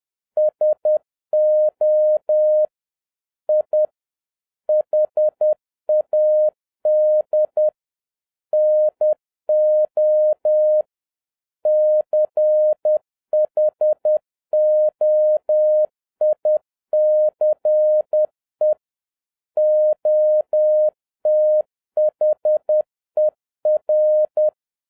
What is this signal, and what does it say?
Morse code, heard as a keyed tone, about 10 wpm, 610 Hz. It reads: SO I HAD NO CHOICE OTHER
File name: 2011-07-mp3 / n41 EXCITING QRP CW OPERATION 10wpm